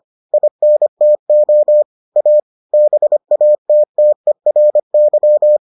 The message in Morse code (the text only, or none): INTO A BATTERY